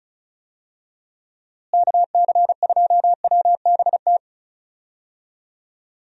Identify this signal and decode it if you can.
Morse code, heard as a keyed tone, about 34 wpm, 700 Hz. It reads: KC2WBT
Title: USA Call Signs - 6 Characters - Repeated 3x - 35wpm